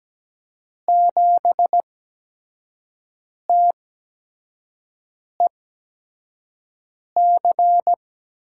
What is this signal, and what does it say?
Morse code, heard as a keyed tone, about 17 wpm, spaced as for 5 wpm, 700 Hz. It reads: 7TEC
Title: USA Call Signs - 8x Character Spacing - 17wpm